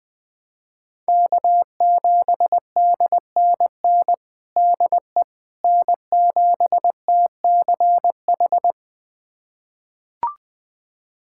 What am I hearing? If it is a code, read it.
Morse code, heard as a keyed tone, about 20 wpm, 700 Hz. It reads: K7DNN DE N7TCH
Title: USA Callsign DE Callsign - 5 Characters - 20wpm